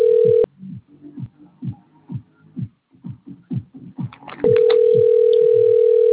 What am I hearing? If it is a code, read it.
Ringback tone (North America), 440 + 480 Hz.